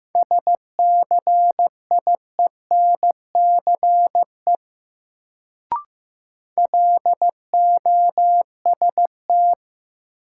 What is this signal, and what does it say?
Morse code, heard as a keyed tone, about 16 wpm, 700 Hz. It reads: SCIENCE LOST
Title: Sentences from Top 1400 Words - New Word Drill Alt - 15wpm